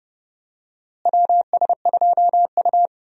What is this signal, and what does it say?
Morse code, heard as a keyed tone, about 29 wpm, 700 Hz. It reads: WS2U
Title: USA Call Signs - 4 Characters - Repeated 2x - 30wpm